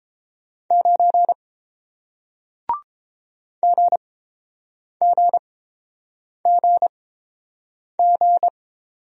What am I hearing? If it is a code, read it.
Morse code, heard as a keyed tone, about 30 wpm, spaced as for 7 wpm, 700 Hz. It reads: 9 GGGG